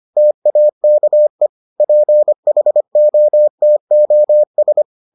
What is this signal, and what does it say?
Morse code, heard as a keyed tone, about 25 wpm, 600 Hz. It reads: TAKE PHOTOS